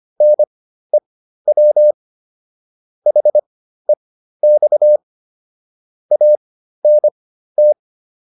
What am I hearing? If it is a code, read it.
Morse code, heard as a keyed tone, about 25 wpm, spaced as for 13 wpm, 600 Hz. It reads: NEW HEX ANT
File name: PR202 13